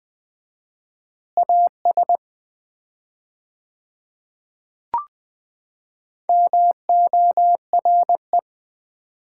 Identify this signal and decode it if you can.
Morse code, heard as a keyed tone, about 20 wpm, 700 Hz. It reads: AS MORE
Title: 4x Word-Spacing - Sets of 3 Words - 20wpm